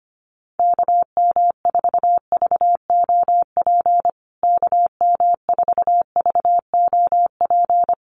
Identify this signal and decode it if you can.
Morse code, heard as a keyed tone, about 25 wpm, 700 Hz. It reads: KM4VOP KM4VOP
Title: USA Call Signs Repeated 2x 25wpm